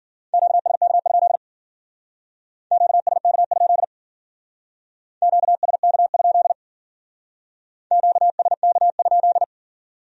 Morse code, heard as a keyed tone, about 51 wpm, 700 Hz. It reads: QSK? QSK? QSK? QSK?